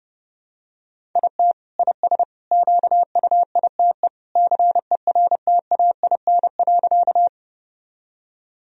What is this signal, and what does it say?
Morse code, heard as a keyed tone, about 32 wpm, 700 Hz. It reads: IT IS QUITE CERTAIN.
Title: Sentences from Top 1000 Words - Review Part 2 - 30wpm